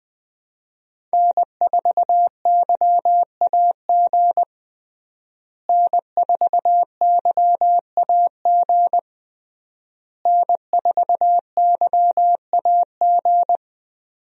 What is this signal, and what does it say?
Morse code, heard as a keyed tone, about 20 wpm, 700 Hz. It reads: N4YAG N4YAG N4YAG
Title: USA Call Signs - 5 Characters - Repeated 3x - 20wpm